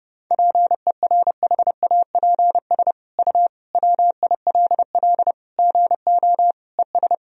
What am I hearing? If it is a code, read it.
Morse code, heard as a keyed tone, about 30 wpm, 700 Hz. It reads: PERHAPS U WILL GO ES
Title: Sentences from Top 500 Words and Abbreviations 30wpm